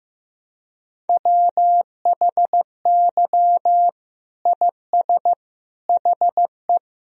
Morse code, heard as a keyed tone, about 15 wpm, 700 Hz. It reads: WHY IS HE
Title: Sentences from Top 300 Words and Abbreviations 15wpm